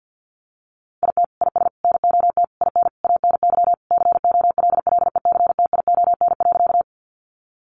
Morse code, heard as a keyed tone, about 50 wpm, 700 Hz. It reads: IT IS NOT IN ANY COLLECTION.
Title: Sentences from Top 1200 Words 50wpm